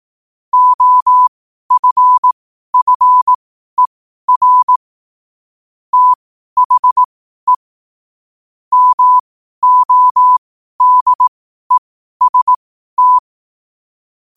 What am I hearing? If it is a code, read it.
Morse code, heard as a keyed tone, about 18 wpm, spaced as for 12 wpm, 1000 Hz. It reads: OFFER THE MODEST
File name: qotdfiles / 12wpm